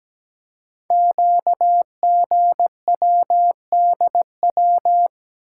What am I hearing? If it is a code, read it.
Morse code, heard as a keyed tone, about 17 wpm, 700 Hz. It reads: QGWDW
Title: Code Groups - 5 Letters - 17wpm